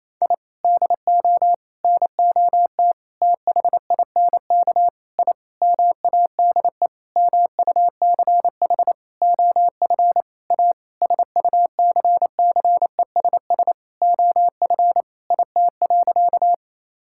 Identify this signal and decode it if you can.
Morse code, heard as a keyed tone, about 28 wpm, 700 Hz. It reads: I DO NOT THINK I MADE MUCH OF A SUCCESS OF IT.
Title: Sentences from Top 1000 Words - Review Part 6 - 28wpm